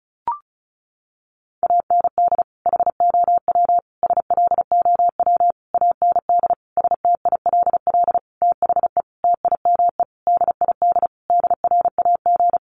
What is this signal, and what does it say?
Morse code, heard as a keyed tone, about 34 wpm, 700 Hz. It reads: AND HOW SLOW AND STILL THE TIME DID DRAG
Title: Sentences from Top 1500 Words 35wpm